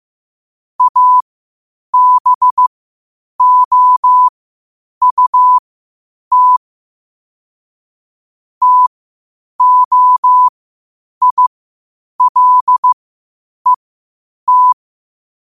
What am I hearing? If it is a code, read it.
Morse code, heard as a keyed tone, about 15 wpm, spaced as for 8 wpm, 1000 Hz. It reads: ABOUT TOILET
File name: 8wpm